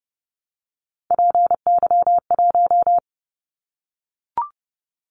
Morse code, heard as a keyed tone, about 30 wpm, 700 Hz. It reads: PY1
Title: Callsign Prefixes for Top 50 DXCC - 30wpm